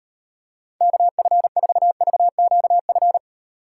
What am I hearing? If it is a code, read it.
Morse code, heard as a keyed tone, about 38 wpm, 700 Hz. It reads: KF4VQF